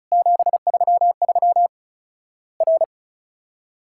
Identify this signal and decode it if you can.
Morse code, heard as a keyed tone, about 36 wpm, 700 Hz. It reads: 733 R
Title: Worked All Europe DX Contest - QTC Exchange - 35wpm